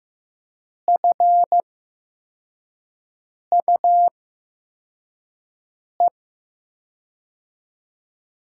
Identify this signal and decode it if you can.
Morse code, heard as a keyed tone, about 15 wpm, spaced as for 4 wpm, 700 Hz. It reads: FUE
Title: USA Call Signs - 6 Characters - 8x Character Spacing - 15wpm